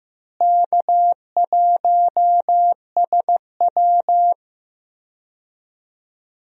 Morse code, heard as a keyed tone, about 15 wpm, 700 Hz. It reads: K1SW
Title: USA Call Signs - 4 Characters - Repeated 3x - 15wpm